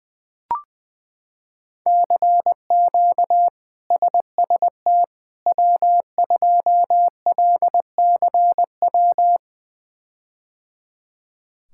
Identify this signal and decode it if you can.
Morse code, heard as a keyed tone, about 20 wpm, 700 Hz. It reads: CQ SST W2LCW
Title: K1USN SST QSO Practice - 20wpm